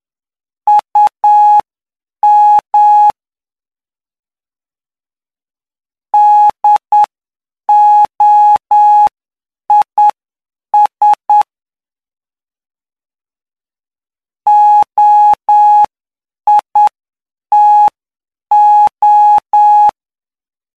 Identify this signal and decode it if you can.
Morse code, heard as a keyed tone, about 9 wpm, spaced as for 7 wpm, 840 Hz. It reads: UM DOIS OITO
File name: cw / py2lcd / 04-PY2LCD-Li��o 13